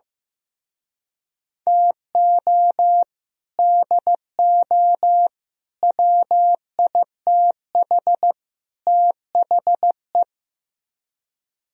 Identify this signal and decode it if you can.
Morse code, heard as a keyed tone, about 17 wpm, 700 Hz. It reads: TO DO WITH THE
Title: Speed-Racing - Sets of 4 Words 15wpm